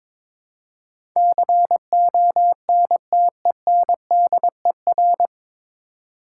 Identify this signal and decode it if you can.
Morse code, heard as a keyed tone, about 22 wpm, 700 Hz. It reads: CONTENDER
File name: 9 Letter Words - First 3 Letters Given - 22wpm